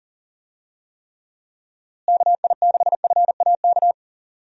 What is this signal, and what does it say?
Morse code, heard as a keyed tone, about 39 wpm, 700 Hz. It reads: KI6FAK
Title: USA Call Signs - 6 Characters - Repeated 3x - 40wpm